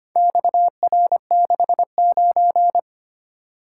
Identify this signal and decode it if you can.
Morse code, heard as a keyed tone, about 25 wpm, 700 Hz. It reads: XR69
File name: practice / 25wpm / Four Letter-Number Combinations - 25wpm